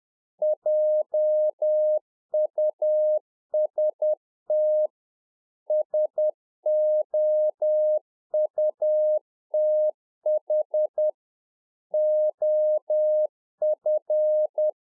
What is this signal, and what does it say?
Morse code, heard as a keyed tone, about 10 wpm, 610 Hz. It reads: JUST SOUTH OF